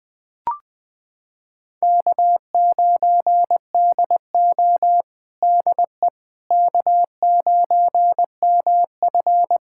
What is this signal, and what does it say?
Morse code, heard as a keyed tone, about 21 wpm, 700 Hz. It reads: K9DO DE K9MF
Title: USA Callsign DE Callsign - 4 Characters - 20wpm